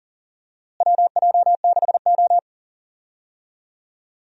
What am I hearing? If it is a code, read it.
Morse code, heard as a keyed tone, about 37 wpm, 700 Hz. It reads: WJ6O
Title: USA Call Signs - 4 Characters - Repeated 2x - 40wpm